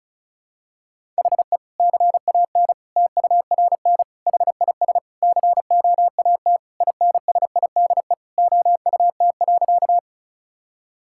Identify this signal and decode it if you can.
Morse code, heard as a keyed tone, about 36 wpm, 700 Hz. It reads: HE CAN TURN HIS COAT INSIDE OUT.
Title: Sentences from Top 1600 Words 35wpm